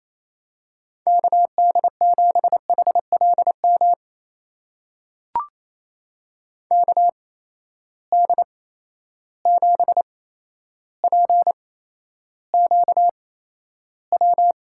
Morse code, heard as a keyed tone, about 28 wpm, spaced as for 12 wpm, 700 Hz. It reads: KD7HLM KD7PQW
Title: USA Call Signs - 6 Characters - 8x Character Spacing - 28wpm